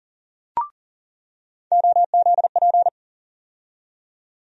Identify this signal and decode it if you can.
Morse code, heard as a keyed tone, about 40 wpm, 700 Hz. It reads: OZP
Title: USA Call Sign Postfixes - 3 Letters 40wpm